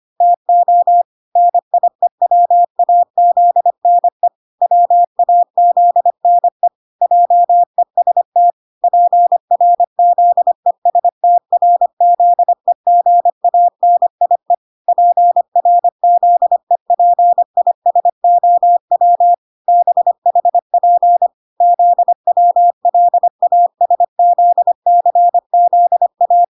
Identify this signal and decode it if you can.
Morse code, heard as a keyed tone, about 25 wpm, 700 Hz. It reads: TO NIEWAZNE WAZNE JEST PRZESTRZEGANIE PRZEPISOW BHP ZWLASZCZA